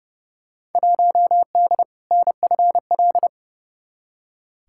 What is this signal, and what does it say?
Morse code, heard as a keyed tone, about 30 wpm, 700 Hz. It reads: 1D NFL
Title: Field Day Exchanges - 30wpm